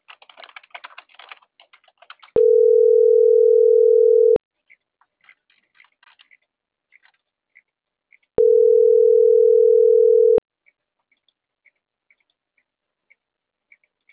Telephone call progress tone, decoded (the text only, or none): ringback tone